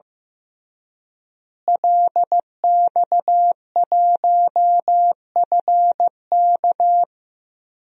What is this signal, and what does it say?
Morse code, heard as a keyed tone, about 15 wpm, 700 Hz. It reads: LX1FK
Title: DXCC-100 Call Signs - 5 Characters - Repeated 2x - 15wpm